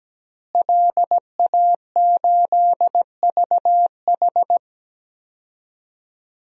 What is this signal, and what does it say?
Morse code, heard as a keyed tone, about 17 wpm, 700 Hz. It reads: LA8VH
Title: DXCC-100 Call Signs - 5 Characters - Repeated 2x - 17wpm